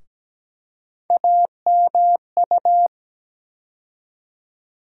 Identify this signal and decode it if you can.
Morse code, heard as a keyed tone, about 17 wpm, 700 Hz. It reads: AMU
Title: USA Call Sign Postfixes - 3 Letters 17wpm